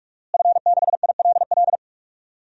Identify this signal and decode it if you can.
Morse code, heard as a keyed tone, about 46 wpm, 700 Hz. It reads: F6ILL
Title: DXCC-100 Call Signs - 5 Characters - Repeated 2x - 45wpm